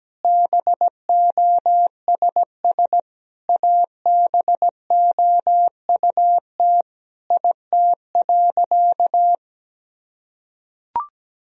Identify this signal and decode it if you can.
Morse code, heard as a keyed tone, about 17 wpm, 700 Hz. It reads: BOSS ABOUT IT.
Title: Sentences from Top 1000 Words - Review Part 4 - 17wpm